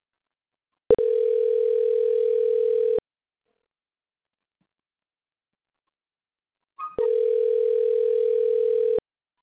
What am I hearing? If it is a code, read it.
Ringback tone (North America), 440 + 480 Hz.